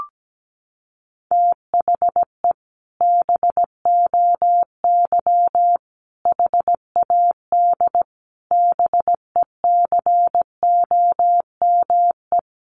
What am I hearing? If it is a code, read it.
Morse code, heard as a keyed tone, about 17 wpm, 700 Hz. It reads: THE BOY HAD BECOME
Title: Sentences from Top 1500 Words 17wpm